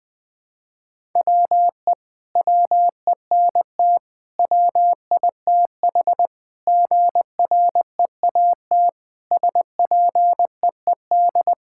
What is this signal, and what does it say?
Morse code, heard as a keyed tone, about 20 wpm, 700 Hz. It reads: WE WENT WITH GREAT SPEED
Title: Sentences from Top 1000 Words - Review Part 5 - 20wpm